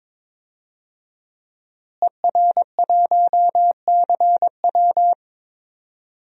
Morse code, heard as a keyed tone, about 21 wpm, 700 Hz. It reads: ER1CW
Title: DXCC-100 Call Signs - 5 Characters - Repeated 3x - 22wpm